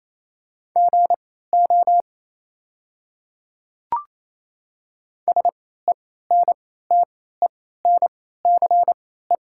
Morse code, heard as a keyed tone, about 27 wpm, spaced as for 16 wpm, 700 Hz. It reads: GO SENTENCE
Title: 3x Character Spacing - Top 300 Words - 28wpm